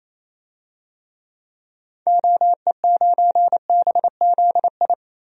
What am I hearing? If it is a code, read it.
Morse code, heard as a keyed tone, about 28 wpm, 700 Hz. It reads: OE9BZI
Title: DXCC-100 Call Signs - 6 Characters - 28wpm